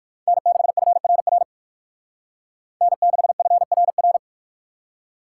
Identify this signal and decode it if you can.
Morse code, heard as a keyed tone, about 51 wpm, 700 Hz. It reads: N6FRR N6FRR
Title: Speed Racing - USA 1x3 Call Signs 45wpm